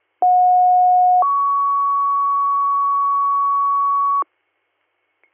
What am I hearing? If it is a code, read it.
Two-tone page, heard as a one-second tone, then a three-second tone, about 730 then 1120 Hz, with noise. It is heard from 0.2 to 4.2 s.
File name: 200- .055 .0-call 6529